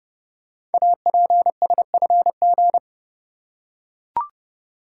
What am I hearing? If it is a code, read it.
Morse code, heard as a keyed tone, about 30 wpm, 700 Hz. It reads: APSFG
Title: Code Groups - 5 Letters - 30wpm